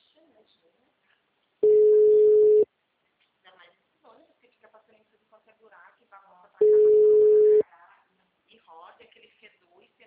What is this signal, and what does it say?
Ringback tone (Europe, 425 Hz), with speech.
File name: out- -2004- - - .24970